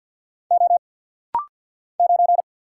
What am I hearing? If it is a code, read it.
Morse code, heard as a keyed tone, about 50 wpm, 700 Hz. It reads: O 9
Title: Single Letter-Number - Rapid-Fire 50wpm